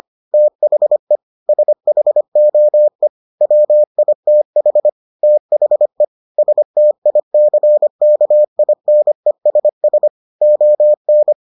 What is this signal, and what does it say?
Morse code, heard as a keyed tone, about 25 wpm, 600 Hz. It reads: THE SHOE WITH THE STICKINESS ON